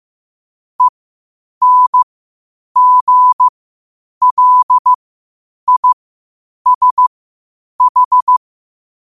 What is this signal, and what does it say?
Morse code, heard as a keyed tone, about 15 wpm, spaced as for 8 wpm, 1000 Hz. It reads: ENGLISH
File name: qotdfiles / 8wpm